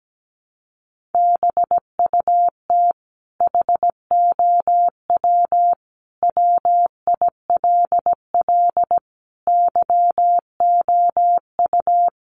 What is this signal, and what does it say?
Morse code, heard as a keyed tone, about 17 wpm, 700 Hz. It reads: BUT HOW WILL YOU
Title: Sentences from Top 1100 Words 17wpm